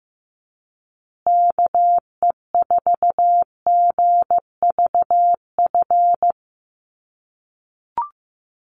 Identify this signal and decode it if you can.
Morse code, heard as a keyed tone, about 15 wpm, 700 Hz. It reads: KE4GVF
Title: USA Call Signs Repeated 3x 15wpm